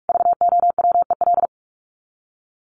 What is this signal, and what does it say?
Morse code, heard as a keyed tone, about 44 wpm, 700 Hz. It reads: VOWEL
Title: Top 500 Words Repeated 45wpm